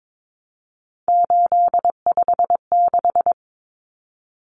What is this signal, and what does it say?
Morse code, heard as a keyed tone, about 22 wpm, 700 Hz. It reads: 856
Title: Numbers - 3-Digit 22wpm